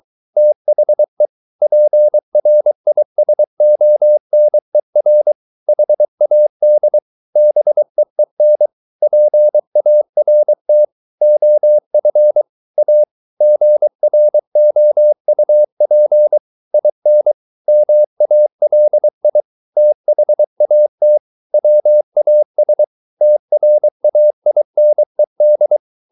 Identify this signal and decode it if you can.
Morse code, heard as a keyed tone, about 23 wpm, 600 Hz. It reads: THE PRISONER HAD BEEN PART OF A GROUP IN MALI THAT WAS TRAINED